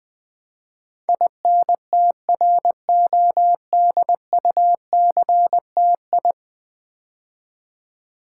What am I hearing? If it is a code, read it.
Morse code, heard as a keyed tone, about 20 wpm, 700 Hz. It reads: INTRODUCTI
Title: Build A Word - 5 Letters At A Time - 12-Letter Words - 20wpm